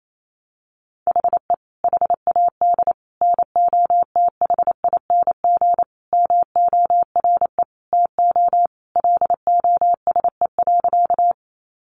Morse code, heard as a keyed tone, about 28 wpm, 700 Hz. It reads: HE HAD NOTHING MORE TO LOSE.
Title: Sentences from Top 1100 Words 28wpm